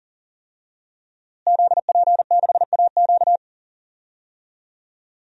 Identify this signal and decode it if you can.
Morse code, heard as a keyed tone, about 40 wpm, 700 Hz. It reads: ZP6AQ
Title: DXCC-100 Call Signs - 5 Characters - Repeated 2x - 40wpm